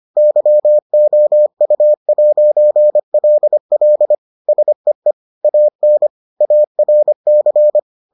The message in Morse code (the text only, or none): YOU'LL SEE AN ARC